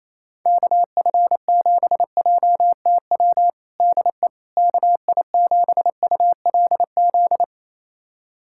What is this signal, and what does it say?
Morse code, heard as a keyed tone, about 28 wpm, 700 Hz. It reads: KF7JTW DE KI7ULZ
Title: USA Callsign DE Callsign 28wpm